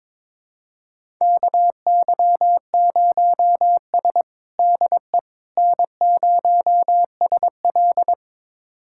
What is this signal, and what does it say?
Morse code, heard as a keyed tone, about 21 wpm, 700 Hz. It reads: KY0S DE N0SL
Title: USA Callsign DE Callsign - 4 Characters - 22wpm